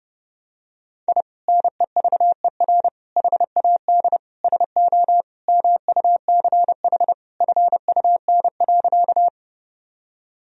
Morse code, heard as a keyed tone, about 32 wpm, 700 Hz. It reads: I NEVER HAD SO MUCH FUN.
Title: Sentences from Top 1000 Words - Review Part 4 - 30wpm